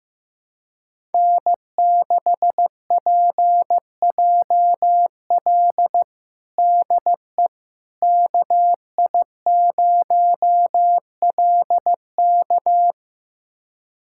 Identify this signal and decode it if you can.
Morse code, heard as a keyed tone, about 16 wpm, 700 Hz. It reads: N6PJL DE KI0LK
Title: USA Callsign DE Callsign - 5 Characters - 15wpm